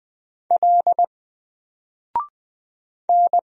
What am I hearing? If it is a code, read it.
Morse code, heard as a keyed tone, about 20 wpm, 700 Hz. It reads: L N
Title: Single Letter-Number Alt 20wpm